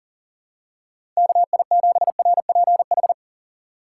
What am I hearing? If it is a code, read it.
Morse code, heard as a keyed tone, about 37 wpm, 700 Hz. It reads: KI7RPH
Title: USA Call Signs - 3x Character Spacing - 40wpm